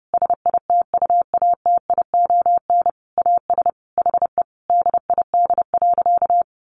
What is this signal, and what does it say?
Morse code, heard as a keyed tone, about 30 wpm, 700 Hz. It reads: SITUATION AS HE DID.
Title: Sentences from Top 1200 Words 30wpm